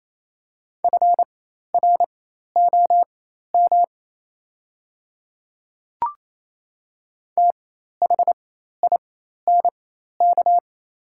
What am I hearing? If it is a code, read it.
Morse code, heard as a keyed tone, about 28 wpm, spaced as for 13 wpm, 690 Hz. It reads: FROM THINK
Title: 4x Character Spacing - Top 200 Words - 28wpm